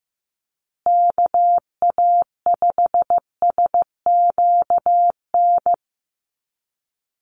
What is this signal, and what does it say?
Morse code, heard as a keyed tone, about 15 wpm, 700 Hz. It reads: KA5SQN
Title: USA Call Signs Repeated 3x 15wpm